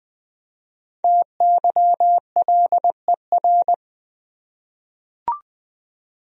Morse code, heard as a keyed tone, about 21 wpm, 700 Hz. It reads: TYLER
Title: USA Common Names - 20wpm